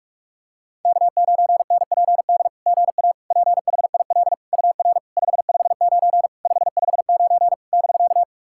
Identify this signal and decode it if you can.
Morse code, heard as a keyed tone, about 44 wpm, 660 Hz. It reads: K9NPD GA PHIL UR 559 559 <BK>
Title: SOTA QSO Practice 45wpm